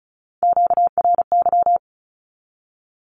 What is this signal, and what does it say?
Morse code, heard as a keyed tone, about 36 wpm, 700 Hz. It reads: QRY
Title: Q-Codes with Spoken Meaning - 35wpm